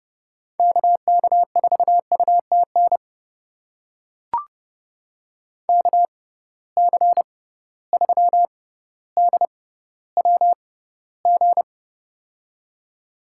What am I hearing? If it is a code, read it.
Morse code, heard as a keyed tone, about 30 wpm, spaced as for 15 wpm, 700 Hz. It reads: KK4UTN KC3DWG
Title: USA Call Signs - 6 Characters - 6x Character Spacing - 30wpm